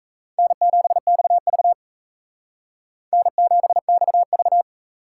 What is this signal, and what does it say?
Morse code, heard as a keyed tone, about 35 wpm, 700 Hz. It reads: N7XV N7XV
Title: Speed Racing - USA Call Signs 28wpm